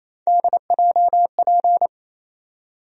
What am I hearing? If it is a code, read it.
Morse code, heard as a keyed tone, about 28 wpm, 700 Hz. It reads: DJP
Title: USA Call Sign Postfixes - 3 Letters 28wpm